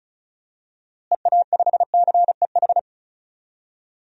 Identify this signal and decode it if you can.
Morse code, heard as a keyed tone, about 34 wpm, 700 Hz. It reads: EA5CEH
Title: DXCC-100 Call Signs - 6 Characters - Repeated 2x - 35wpm